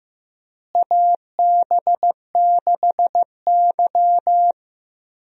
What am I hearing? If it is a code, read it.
Morse code, heard as a keyed tone, about 15 wpm, 700 Hz. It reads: AB6Y
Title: USA Call Signs - 4 Characters - Repeated 2x - 15wpm